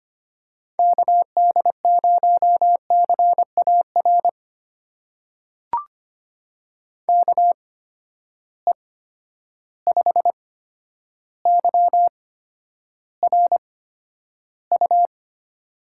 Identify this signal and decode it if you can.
Morse code, heard as a keyed tone, about 25 wpm, spaced as for 11 wpm, 700 Hz. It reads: KD0CAR KE5YRU
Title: USA Call Signs - 6 Characters - 8x Character Spacing - 25wpm